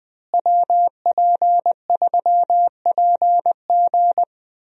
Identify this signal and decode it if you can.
Morse code, heard as a keyed tone, about 21 wpm, 700 Hz. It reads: WP3PG
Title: USA Call Signs - 5 Characters - Repeated 3x - 20wpm